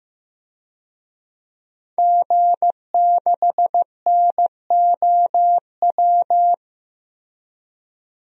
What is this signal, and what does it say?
Morse code, heard as a keyed tone, about 15 wpm, 700 Hz. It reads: G6NOW
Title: DXCC-100 Call Signs - 5 Characters - Repeated 3x - 15wpm